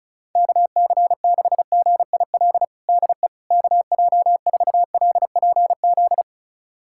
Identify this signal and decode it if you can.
Morse code, heard as a keyed tone, about 34 wpm, 700 Hz. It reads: KC6GIL DE KJ4LPZ
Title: USA Callsign DE Callsign - 6 Characters - 35wpm